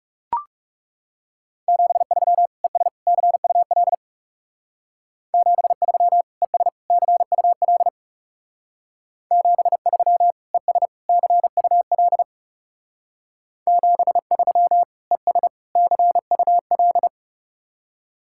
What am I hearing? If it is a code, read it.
Morse code, heard as a keyed tone, about 38 wpm, 700 Hz. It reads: 73 ES CUL 73 ES CUL 73 ES CUL 73 ES CUL